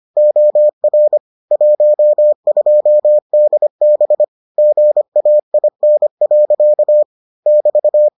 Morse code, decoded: OR 12DB GAIN. <BT>

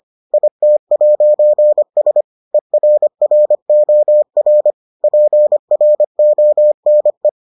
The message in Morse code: IT'S ERROR PRONE

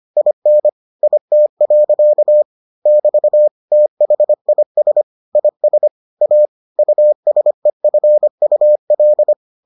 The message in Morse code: IN IT. <BT> THIS IS A USEFUL

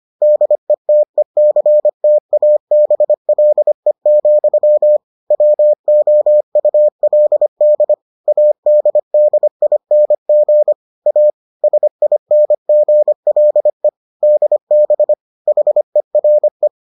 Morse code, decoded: DETECTABLE, WOULD ADDING A SINGLE DB HERE

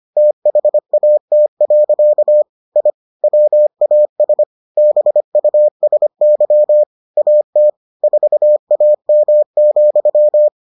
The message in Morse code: THAT. I WAS BUSY AT 4AM,

